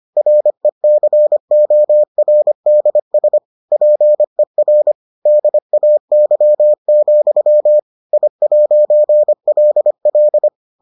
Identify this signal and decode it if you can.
Morse code, heard as a keyed tone, about 25 wpm, 600 Hz. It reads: RECORDS PER DAY, I'LL